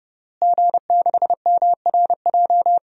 Morse code, heard as a keyed tone, about 30 wpm, 700 Hz. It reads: G6MRJ
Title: DXCC-100 Call Signs - 5 Characters - Repeated 3x - 30wpm